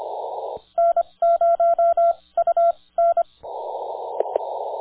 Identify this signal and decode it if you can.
Morse code, heard as a keyed tone, about 25 wpm, spaced as for 19 wpm, 1360 Hz, with radio noise. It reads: N0UN
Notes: Viet Nam on 30 Meters CW at 9:00 AM our local time